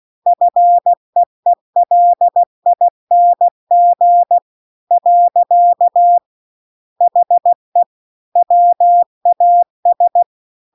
Morse code, heard as a keyed tone, about 16 wpm, 700 Hz. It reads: FEELING . HE WAS